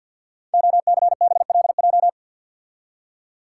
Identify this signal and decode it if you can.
Morse code, heard as a keyed tone, about 52 wpm, 700 Hz. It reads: OCBLJ
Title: Code Groups - 5 Letters - 50wpm